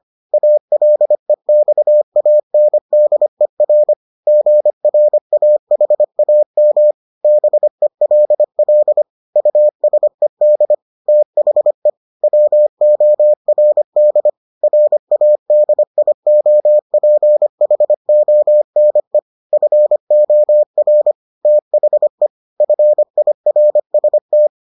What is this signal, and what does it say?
Morse code, heard as a keyed tone, about 25 wpm, 600 Hz. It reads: ALEXANDER GRAHAM BELL USED THE WORD RADIOPHONE FOR THE FIRST